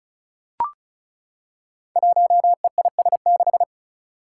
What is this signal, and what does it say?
Morse code, heard as a keyed tone, about 36 wpm, 700 Hz. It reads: 1EIS6
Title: Five Challenging Letter-Number Combinations 35wpm